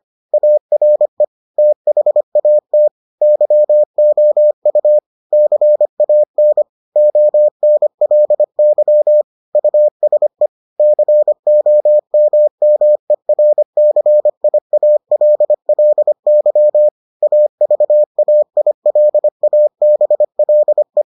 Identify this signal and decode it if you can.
Morse code, heard as a keyed tone, about 25 wpm, 600 Hz. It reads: ARE THAT YOU CAN ONLY USE COMMERCIALLY AVAILABLE